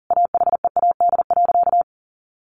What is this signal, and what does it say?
Morse code, heard as a keyed tone, about 40 wpm, 700 Hz. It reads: AHEAD.
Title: Sentences from Top 1100 Words 40wpm